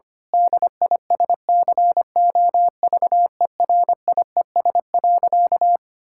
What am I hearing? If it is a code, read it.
Morse code, heard as a keyed tone, about 25 wpm, 700 Hz. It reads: DISCOVERIES.
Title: Pangram Challenge v2 - 25wpm